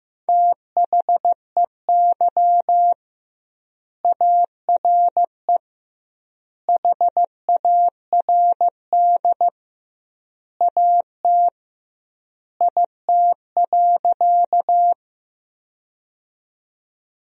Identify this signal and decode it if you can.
Morse code, heard as a keyed tone, about 15 wpm, 700 Hz. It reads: THEY ARE HARD AT IT.
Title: 2x Word-Spacing - Sentences from Top 300 Words - 15wpm